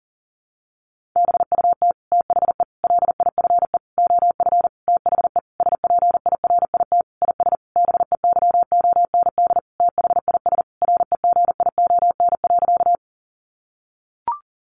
Morse code, heard as a keyed tone, about 40 wpm, 700 Hz. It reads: BUT THE LIFE OF THE SPIRIT IS BEYOND THIS REGION.
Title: Sentences from Top 1500 Words 40wpm